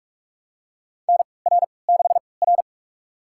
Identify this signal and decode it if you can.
Morse code, heard as a keyed tone, about 46 wpm, spaced as for 25 wpm, 700 Hz. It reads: NR6R
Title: USA Call Signs - 4 Characters - 3x Character Spacing - 45wpm